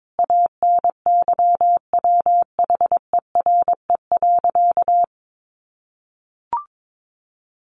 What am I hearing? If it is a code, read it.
Morse code, heard as a keyed tone, about 22 wpm, 700 Hz. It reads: ANYWHERE.
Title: Sentences from Top 1400 Words 22wpm